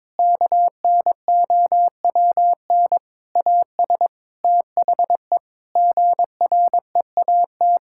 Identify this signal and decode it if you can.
Morse code, heard as a keyed tone, about 22 wpm, 700 Hz. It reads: KNOWN AS THE GREAT
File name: practice / 22wpm / Sentences with 8-Word Phrases - 22wpm